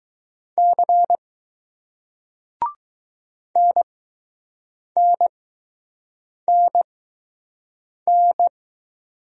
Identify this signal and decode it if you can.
Morse code, heard as a keyed tone, about 20 wpm, spaced as for 6 wpm, 700 Hz. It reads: C NNNN